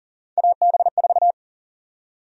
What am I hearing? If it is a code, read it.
Morse code, heard as a keyed tone, about 40 wpm, 700 Hz. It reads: AB4
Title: USA Call Sign Prefixes - 40wpm